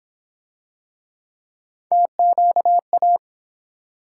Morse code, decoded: TQA

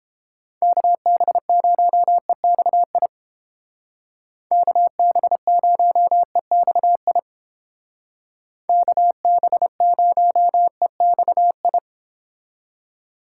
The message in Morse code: KB0EXI KB0EXI KB0EXI

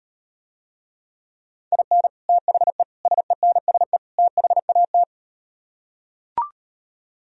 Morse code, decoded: IN THE SENSE THAT

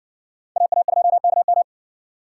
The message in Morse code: AA2KM

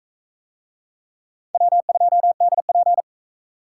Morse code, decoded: W2DP

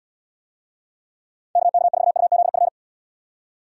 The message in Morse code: KK4UZF

